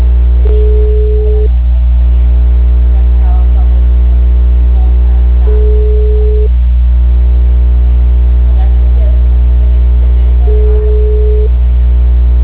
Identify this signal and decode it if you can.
Ringback tone (Europe, 425 Hz).
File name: out- -2006- - - .48998